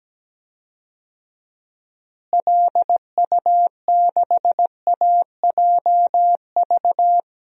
Morse code, heard as a keyed tone, about 17 wpm, 700 Hz. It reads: LU6AJV
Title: DXCC-100 Call Signs - 6 Characters - Repeated 2x - 17wpm